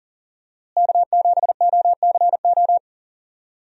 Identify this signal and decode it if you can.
Morse code, heard as a keyed tone, about 40 wpm, 700 Hz. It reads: K7OCO
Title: USA Call Signs - 5 Characters - Repeated 3x - 40wpm